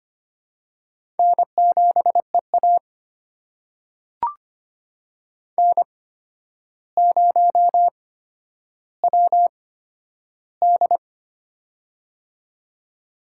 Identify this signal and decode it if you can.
Morse code, heard as a keyed tone, about 25 wpm, spaced as for 11 wpm, 700 Hz. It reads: N7EA N0WD
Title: USA Call Signs - 4 Characters - 8x Character Spacing - 25wpm